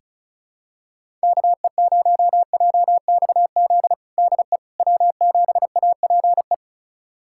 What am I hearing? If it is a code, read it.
Morse code, heard as a keyed tone, about 36 wpm, 700 Hz. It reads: KE0JXZ DE W7APE
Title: USA Callsign DE Callsign 35wpm